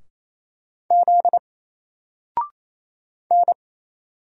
Morse code, heard as a keyed tone, about 27 wpm, 700 Hz. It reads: Z N
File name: Single-Letter-Number-28wpm